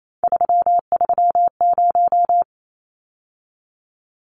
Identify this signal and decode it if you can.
Morse code, heard as a keyed tone, about 28 wpm, 700 Hz. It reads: 330
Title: Numbers - 3-Digit 28wpm